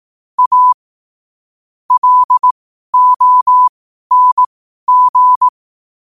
Morse code, heard as a keyed tone, about 18 wpm, spaced as for 12 wpm, 1000 Hz. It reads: A LONG
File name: qotdfiles / 12wpm